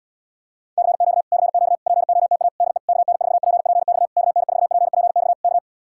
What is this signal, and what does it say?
Morse code, heard as a keyed tone, about 77 wpm, 700 Hz. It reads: CQ CQ POTA DE KN4PPX KN4PPX K